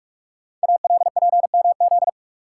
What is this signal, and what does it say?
Morse code, heard as a keyed tone, about 44 wpm, 700 Hz. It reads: ALPMZ
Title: Code Groups - 5 Letter:Numbers - 45wpm